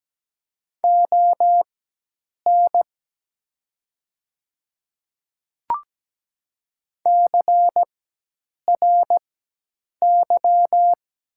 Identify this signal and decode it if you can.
Morse code, heard as a keyed tone, about 17 wpm, spaced as for 8 wpm, 700 Hz. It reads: ON CRY
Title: 4x Character Spacing - Top 500 Words - 17wpm